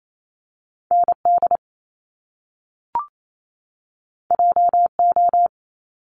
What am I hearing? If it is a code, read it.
Morse code, heard as a keyed tone, about 28 wpm, 700 Hz. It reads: ND JO
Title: Top Two Letter Combinations in Words 28wpm